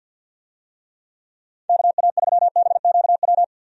Morse code, heard as a keyed tone, about 51 wpm, 700 Hz. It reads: KA3BQW
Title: USA Call Signs - 6 Characters - 50wpm